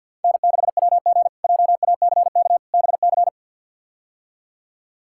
Morse code, heard as a keyed tone, about 47 wpm, 700 Hz. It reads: N6WG JACK BC
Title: K1USN SST QSO Practice - 50wpm